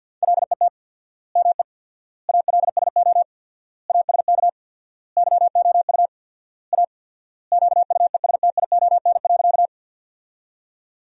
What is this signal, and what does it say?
Morse code, heard as a keyed tone, about 50 wpm, 700 Hz. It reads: LET ME ALSO ASK YOU A QUESTION.
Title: 4x Word-Spacing - Sentences from Top 400 Words - 50wpm